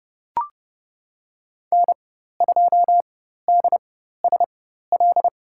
Morse code, heard as a keyed tone, about 30 wpm, spaced as for 14 wpm, 700 Hz. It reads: N2DSL
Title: USA Call Signs - 5 Characters - 4x Character Spacing - 30wpm